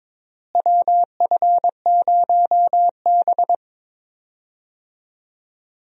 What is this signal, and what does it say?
Morse code, heard as a keyed tone, about 22 wpm, 700 Hz. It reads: WF0B
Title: USA Call Signs - 4 Characters - Repeated 3x - 22wpm